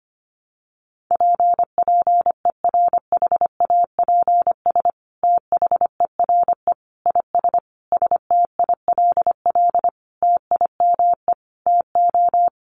Morse code, heard as a keyed tone, about 25 wpm, 700 Hz. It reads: PPERHAPS THERE IS STILL TIME TO
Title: Sentences from Top 1100 Words 25wpm